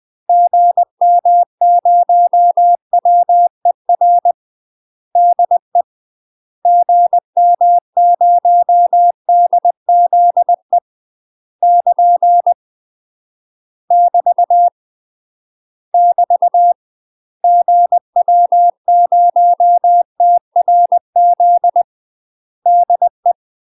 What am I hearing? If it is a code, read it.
Morse code, heard as a keyed tone, about 20 wpm, 700 Hz. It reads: GM0WER DE GM0DZE <KN> <BT> <BT> GW0TRZ DE
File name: UK MASTER QSOs TRAINING / 20WPM / UK MASTER QSOs 0068